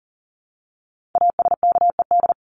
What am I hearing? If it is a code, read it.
Morse code, heard as a keyed tone, about 40 wpm, 700 Hz. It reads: ASKED
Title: ED Suffix - Top 100 Words with Base Words 40wpm